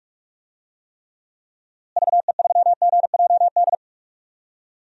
Morse code, heard as a keyed tone, about 46 wpm, 700 Hz. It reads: VE3GJD
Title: DXCC-100 Call Signs - 6 Characters - Repeated 2x - 45wpm